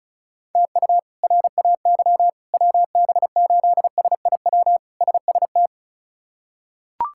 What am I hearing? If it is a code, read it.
Morse code, heard as a keyed tone, about 36 wpm, 700 Hz. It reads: TU RAY WB8SIW SST
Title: K1USN SST QSO Practice - 35wpm